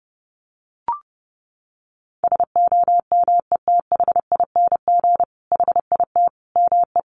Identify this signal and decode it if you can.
Morse code, heard as a keyed tone, about 30 wpm, 700 Hz. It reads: SOMETHING HIT ME
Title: Sentences from Top 1200 Words 30wpm